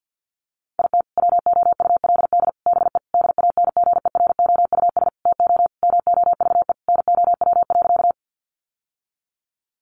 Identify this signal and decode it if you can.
Morse code, heard as a keyed tone, about 50 wpm, 700 Hz. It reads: IT WOULD BE DANGEROUS TO MOVE NOW.
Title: Sentences from Top 1200 Words 50wpm